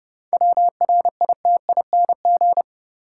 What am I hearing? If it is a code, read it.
Morse code, heard as a keyed tone, about 31 wpm, 700 Hz. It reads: WRITING
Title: Sentences from Top 800 Words - New Word Drill - 30wpm